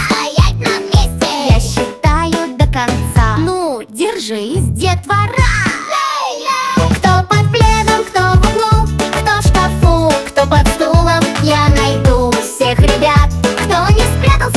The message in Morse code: none